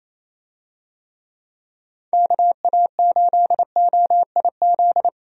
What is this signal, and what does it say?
Morse code, heard as a keyed tone, about 28 wpm, 700 Hz. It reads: KA8OIZ
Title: USA Call Signs - 6 Characters - Repeated 3x - 28wpm